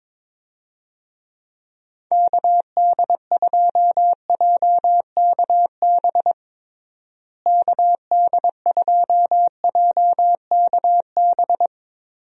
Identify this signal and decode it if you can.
Morse code, heard as a keyed tone, about 22 wpm, 700 Hz. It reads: KD2JKB KD2JKB
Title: USA Call Signs - 6 Characters - Repeated 2x - 22wpm